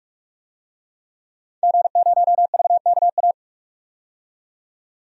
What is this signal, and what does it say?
Morse code, heard as a keyed tone, about 47 wpm, 700 Hz. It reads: G0VKA
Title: DXCC-100 Call Signs - 5 Characters - 45wpm